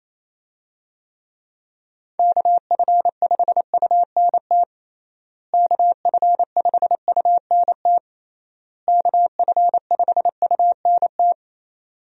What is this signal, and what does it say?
Morse code, heard as a keyed tone, about 27 wpm, 700 Hz. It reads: KF5UNT KF5UNT KF5UNT
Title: USA Call Signs - 6 Characters - Repeated 3x - 28wpm